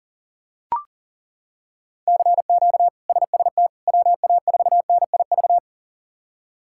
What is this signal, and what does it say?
Morse code, heard as a keyed tone, about 40 wpm, 700 Hz. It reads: CQ SST WA4NIV
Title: K1USN SST QSO Practice - 40wpm